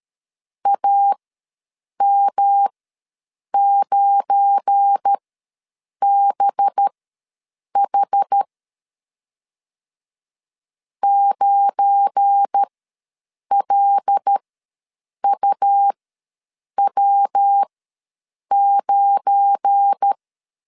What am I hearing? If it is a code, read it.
Morse code, heard as a keyed tone, about 13 wpm, spaced as for 7 wpm, 800 Hz. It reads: AM9BH 9LUW9
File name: LEZIONI / 7 GF9Z / 7 GF9 ws 12wpm 3344